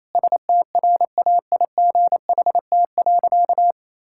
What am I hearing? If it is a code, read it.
Morse code, heard as a keyed tone, about 28 wpm, 700 Hz. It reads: STRAIGHT.
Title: Sentences from Top 1000 Words - Review Part 6 - 28wpm